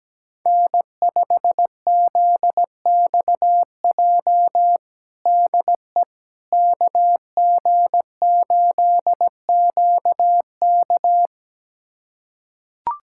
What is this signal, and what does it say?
Morse code, heard as a keyed tone, about 17 wpm, 700 Hz. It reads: N5ZXJ DE KG8QK
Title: USA Callsign DE Callsign - 5 Characters - 17wpm